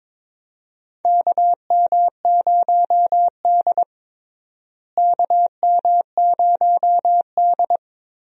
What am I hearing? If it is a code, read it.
Morse code, heard as a keyed tone, about 22 wpm, 700 Hz. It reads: KM0D KM0D
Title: USA Call Signs - 4 Characters - Repeated 3x - 22wpm